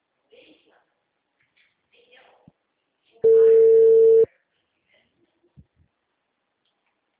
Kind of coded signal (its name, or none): telephone call progress tone